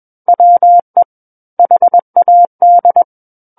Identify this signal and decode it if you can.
Morse code, heard as a keyed tone, about 21 wpm, 700 Hz. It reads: WE HAD
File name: Book One / 12-Chapter 12 - 21 WPM